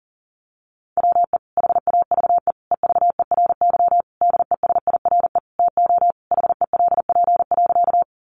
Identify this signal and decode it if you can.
Morse code, heard as a keyed tone, about 40 wpm, 700 Hz. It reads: WE HAVE EVERY DESIRE TO HELP.
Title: Sentences from Top 1100 Words 40wpm